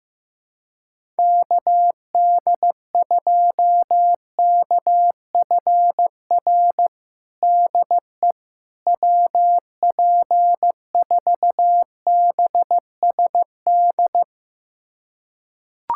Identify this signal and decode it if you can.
Morse code, heard as a keyed tone, about 15 wpm, 700 Hz. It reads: KD2KFR DE WP4BSD
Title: USA Callsign DE Callsign 15wpm